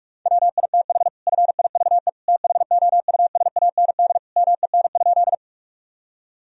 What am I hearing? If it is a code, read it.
Morse code, heard as a keyed tone, about 46 wpm, 700 Hz. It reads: WITH FIVE THOUSAND MEN?
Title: Sentences from Top 1000 Words - Review Part 3 - 45wpm